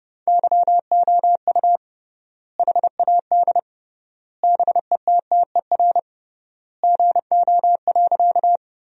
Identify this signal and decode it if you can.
Morse code, heard as a keyed tone, about 28 wpm, 700 Hz. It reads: YOU HAD BETTER GO.
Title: 3x Word-Spacing - Sentences from Top 400 Words - 30wpm